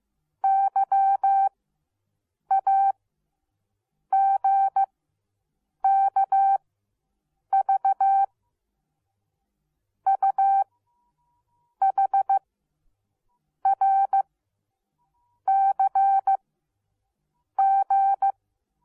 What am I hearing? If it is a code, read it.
Morse code, heard as a keyed tone, about 15 wpm, spaced as for 6 wpm, 780 Hz. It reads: YAGKV UHRCG